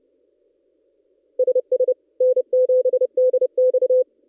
Morse code, heard as a keyed tone, about 29 wpm, 510 Hz. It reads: SS N7DX
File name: contests / ss-cw-k5tr-op-n5rz -1700